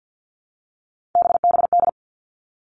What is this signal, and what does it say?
Morse code, heard as a keyed tone, about 50 wpm, 700 Hz. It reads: BBD
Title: All Cut Numbers - 3-Digit 50wpm